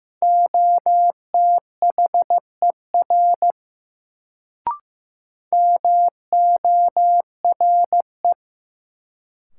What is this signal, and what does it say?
Morse code, heard as a keyed tone, about 15 wpm, 700 Hz. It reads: OTHER MORE
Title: Sentences from Top 100 Words - New Word Drill - 15wpm